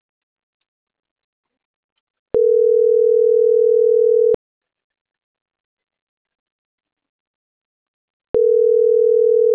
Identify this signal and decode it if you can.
Ringback tone (North America), 440 + 480 Hz.